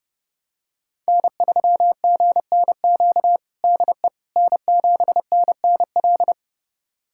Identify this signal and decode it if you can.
Morse code, heard as a keyed tone, about 30 wpm, 700 Hz. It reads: N3GNQ DE N7NNL
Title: USA Callsign DE Callsign - 5 Characters - 30wpm